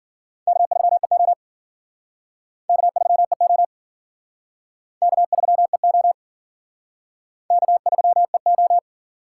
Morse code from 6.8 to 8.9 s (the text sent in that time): K3EO